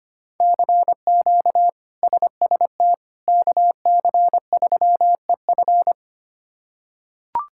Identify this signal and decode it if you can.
Morse code, heard as a keyed tone, about 25 wpm, 700 Hz. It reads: CQ SST KC3EF
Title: K1USN SST QSO Practice - 25wpm